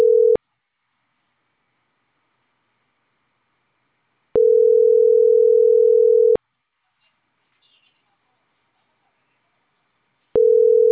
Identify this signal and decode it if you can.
Ringback tone (North America), 440 + 480 Hz.